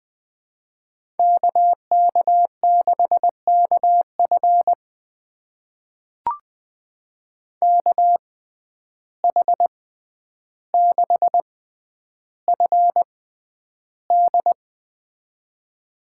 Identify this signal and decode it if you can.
Morse code, heard as a keyed tone, about 20 wpm, spaced as for 10 wpm, 700 Hz. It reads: KK6KF KH6FD
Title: USA Call Signs - 6x Character Spacing - 20wpm